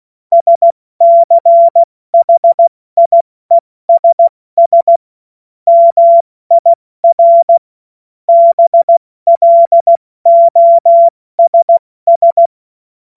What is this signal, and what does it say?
Morse code, heard as a keyed tone, about 16 wpm, spaced as for 14 wpm, 680 Hz. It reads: SCHIESS MIR BLOSS
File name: Morsekurs / Witze / Witz 07. September 2021, 70 F80